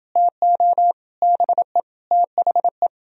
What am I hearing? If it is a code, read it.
Morse code, heard as a keyed tone, about 30 wpm, 700 Hz. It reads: TO BE THE